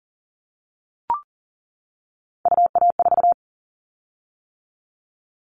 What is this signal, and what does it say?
Morse code, heard as a keyed tone, about 40 wpm, 700 Hz. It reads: UA4
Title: Callsign Prefixes for Top 100 DXCC - 40wpm